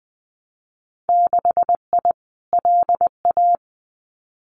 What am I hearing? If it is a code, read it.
Morse code, heard as a keyed tone, about 20 wpm, 700 Hz. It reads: 6I LA
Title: Winter Field Day Exchanges Encore 20wpm